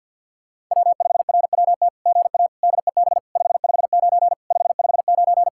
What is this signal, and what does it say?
Morse code, heard as a keyed tone, about 46 wpm, 700 Hz. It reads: W5RWT GA DEB 559 559
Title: SOTA QSO Practice 50wpm